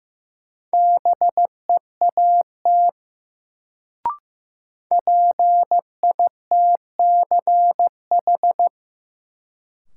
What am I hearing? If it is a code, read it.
Morse code, heard as a keyed tone, about 15 wpm, 700 Hz. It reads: BEAT PITCH
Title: Sentences from Top 1300 Words - New Word Drill - 15wpm